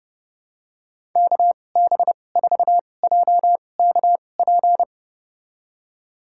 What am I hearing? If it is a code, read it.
Morse code, heard as a keyed tone, about 32 wpm, spaced as for 22 wpm, 700 Hz. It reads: KB4JKP
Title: USA Call Signs - 6 Characters - 2x Character Spacing - 30wpm